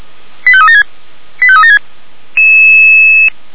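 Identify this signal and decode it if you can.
Five-tone selective call (ZVEI), heard as quick runs of five tones, tones of about 70 ms. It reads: 85376 85376
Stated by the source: Five-tone ZVEI selective call received on an SDR.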